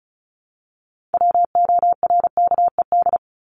Morse code, heard as a keyed tone, about 34 wpm, 700 Hz. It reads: WORKED
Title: ED Suffix - Top 100 Words with Base Words 35wpm